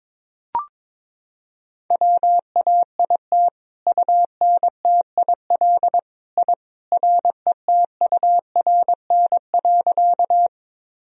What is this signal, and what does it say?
Morse code, heard as a keyed tone, about 22 wpm, 700 Hz. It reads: WAIT UNTIL I RETURN.
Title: Sentences from Top 1000 Words - Review Part 1 - 22wpm